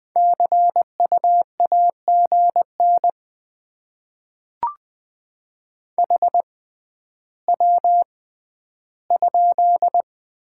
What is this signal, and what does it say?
Morse code, heard as a keyed tone, about 20 wpm, spaced as for 11 wpm, 690 Hz. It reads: CU AGN HW?
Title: Top 20 QSO Elements - 6x Character Spacing - 20wpm